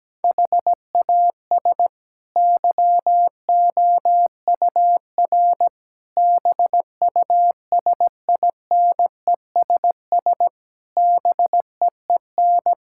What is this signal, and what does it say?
Morse code, heard as a keyed tone, about 17 wpm, 700 Hz. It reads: HAS YOUR BUSINESS BEEN
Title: Sentences from Top 1000 Words - Review Part 6 - 17wpm